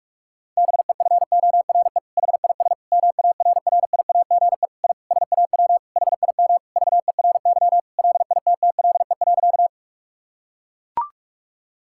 Morse code, heard as a keyed tone, about 41 wpm, 700 Hz. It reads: BEFORE HIS MARRIAGE I SAW HIM VERY LITTLE.
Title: Sentences from Top 1000 Words - Review Part 5 - 45wpm